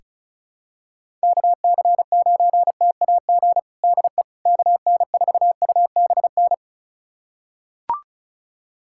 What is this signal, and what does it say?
Morse code, heard as a keyed tone, about 34 wpm, 700 Hz. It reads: KC9TAG DE KN4UBN
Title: USA Callsign DE Callsign - 6 Characters - 35wpm